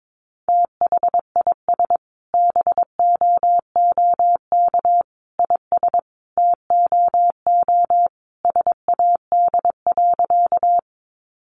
Morse code, heard as a keyed tone, about 22 wpm, 700 Hz. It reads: THIS BOOK IS TOO SAD.
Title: Additional Easy Phrases 22wpm